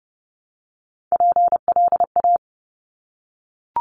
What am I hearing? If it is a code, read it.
Morse code, heard as a keyed tone, about 30 wpm, 700 Hz. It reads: PLA
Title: Top Three Letter Combinations in Words 30wpm